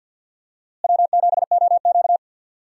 Morse code, heard as a keyed tone, about 47 wpm, 700 Hz. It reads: W7OQ
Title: USA Call Signs - 4 Characters - Repeated 3x - 50wpm